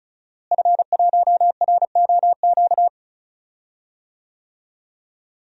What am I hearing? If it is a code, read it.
Morse code, heard as a keyed tone, about 34 wpm, 700 Hz. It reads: F1ROQ
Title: DXCC-100 Call Signs - 5 Characters - Repeated 3x - 35wpm